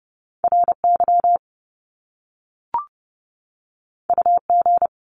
Morse code, heard as a keyed tone, about 30 wpm, 700 Hz. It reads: RY UG
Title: Top Two Letter Combinations in Words 30wpm